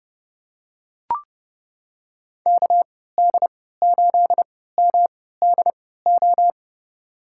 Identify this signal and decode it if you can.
Morse code, heard as a keyed tone, about 30 wpm, spaced as for 17 wpm, 700 Hz. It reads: KD8MDO
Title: USA Call Signs - 3x Character Spacing - 30wpm